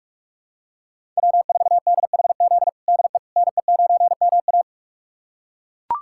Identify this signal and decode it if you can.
Morse code, heard as a keyed tone, about 41 wpm, 700 Hz. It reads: W4DHZ DE NE9MA
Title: USA Callsign DE Callsign - 5 Characters - 45wpm